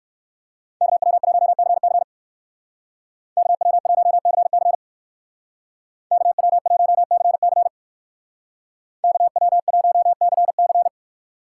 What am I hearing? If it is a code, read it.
Morse code, heard as a keyed tone, about 58 wpm, 700 Hz. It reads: KW1CC KW1CC KW1CC KW1CC